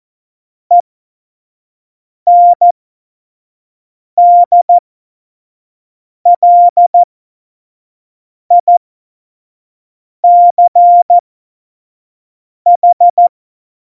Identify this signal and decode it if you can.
Morse code, heard as a keyed tone, about 14 wpm, spaced as for 5 wpm, 700 Hz. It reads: ENDLICH
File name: Morsekurs / Witze / Witz 18. September 2018, 25 F70